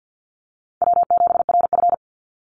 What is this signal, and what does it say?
Morse code, heard as a keyed tone, about 48 wpm, 700 Hz. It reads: W7RF
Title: Easy Call Signs Repeated 2x - 50wpm